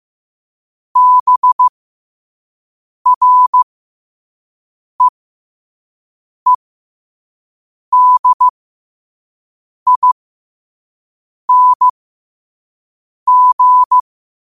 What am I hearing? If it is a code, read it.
Morse code, heard as a keyed tone, about 15 wpm, spaced as for 5 wpm, 1000 Hz. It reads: BREEDING
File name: qotdfiles / 5wpm